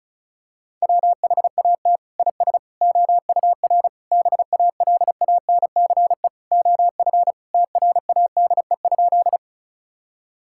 Morse code, heard as a keyed tone, about 36 wpm, 700 Hz. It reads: WHAT IS OUR BALANCE OF TRADE?
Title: Sentences from Top 1000 Words - Review Part 6 - 35wpm